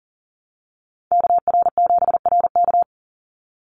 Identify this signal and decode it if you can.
Morse code, heard as a keyed tone, about 40 wpm, 700 Hz. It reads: KR7RK
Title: NA SOTA Chaser Callsigns 40wpm